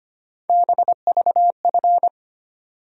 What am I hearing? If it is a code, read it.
Morse code, heard as a keyed tone, about 25 wpm, 700 Hz. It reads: BVF
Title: USA Call Sign Postfixes - 3 Letters 25wpm